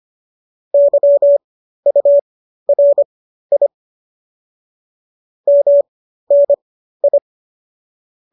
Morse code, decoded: YURI MNI

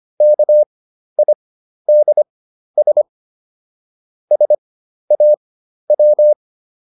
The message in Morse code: KIDS SAW

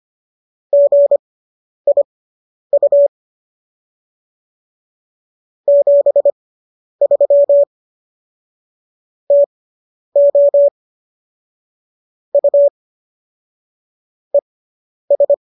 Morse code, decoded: GIU 73 TO U ES

